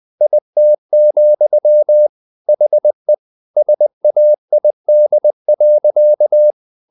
Morse code, heard as a keyed tone, about 20 wpm, 600 Hz. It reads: IT, HE SAID.
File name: the old man and the sea / the old man and the sea part 1 20wpm